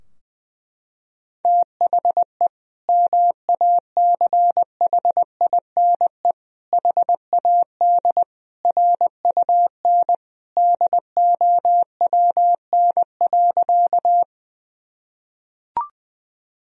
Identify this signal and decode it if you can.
Morse code, heard as a keyed tone, about 20 wpm, 700 Hz. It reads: THE MACHINE HAD RUN DOWN.
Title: Sentences from Top 1000 Words - Review Part 2 - 20wpm